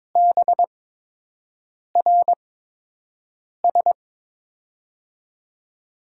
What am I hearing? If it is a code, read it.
Morse code, heard as a keyed tone, about 22 wpm, spaced as for 6 wpm, 700 Hz. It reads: BRS
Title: USA Call Signs - 5 Characters - 8x Character Spacing - 22wpm